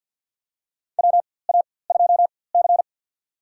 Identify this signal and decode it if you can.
Morse code, heard as a keyed tone, about 53 wpm, spaced as for 24 wpm, 700 Hz. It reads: WA2C
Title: USA Call Signs - 4x Character Spacing - 50wpm